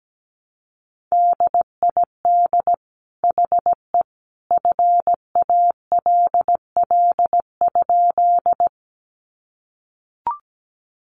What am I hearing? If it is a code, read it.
Morse code, heard as a keyed tone, about 17 wpm, 700 Hz. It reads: DID HE FALL?
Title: Sentences from Top 500 Words 17wpm